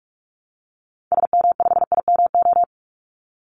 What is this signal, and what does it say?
Morse code, heard as a keyed tone, about 44 wpm, 700 Hz. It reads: SM5IMO
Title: Call Signs Repeated 3x - 45wpm